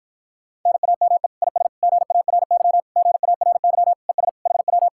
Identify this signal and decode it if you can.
Morse code, heard as a keyed tone, about 46 wpm, 700 Hz. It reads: NAME IS GARY GARY ES HW